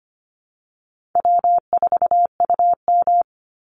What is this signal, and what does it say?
Morse code, heard as a keyed tone, about 25 wpm, 700 Hz. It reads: W4UM
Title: Easy Call Signs Repeated 2x - 25wpm